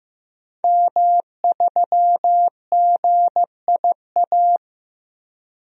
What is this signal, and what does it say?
Morse code, heard as a keyed tone, about 15 wpm, 700 Hz. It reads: M3GIA
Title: DXCC-100 Call Signs - 5 Characters - Repeated 2x - 15wpm